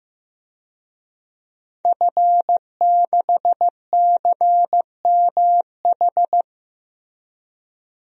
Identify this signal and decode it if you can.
Morse code, heard as a keyed tone, about 16 wpm, 700 Hz. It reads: F6CMH
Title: DXCC-100 Call Signs - 5 Characters - Repeated 3x - 15wpm